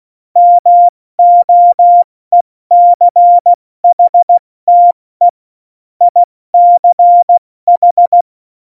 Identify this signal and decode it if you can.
Morse code, heard as a keyed tone, about 16 wpm, 700 Hz. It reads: MOECHTE ICH
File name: Morsekurs / Witze / Witz 08. August 17, Tempo 70 F80